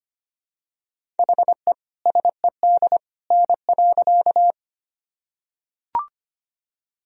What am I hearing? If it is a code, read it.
Morse code, heard as a keyed tone, about 25 wpm, 700 Hz. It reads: HE SED N.
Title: Sentences from Top 100 Words and Abbreviations 25wpm